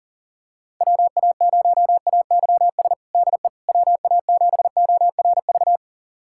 Morse code, heard as a keyed tone, about 40 wpm, 700 Hz. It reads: WA0AYS DE WA7ORV
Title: USA Callsign DE Callsign - 6 Characters - 40wpm